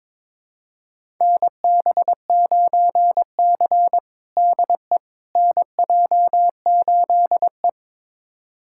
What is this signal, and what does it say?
Morse code, heard as a keyed tone, about 22 wpm, 700 Hz. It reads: NB9C DE NJ8E
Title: USA Callsign DE Callsign - 4 Characters - 22wpm